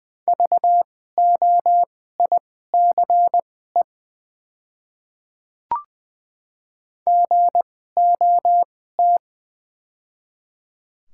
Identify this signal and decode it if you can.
Morse code, heard as a keyed tone, about 20 wpm, spaced as for 14 wpm, 700 Hz. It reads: VOICE GOT
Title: 2x Character Spacing - Top 500 Words - 20wpm